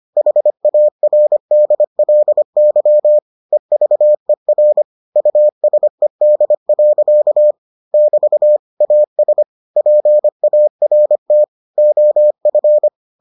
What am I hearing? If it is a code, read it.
Morse code, heard as a keyed tone, about 25 wpm, 600 Hz. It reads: HARDLY EVER USED. <BT> AS PART OF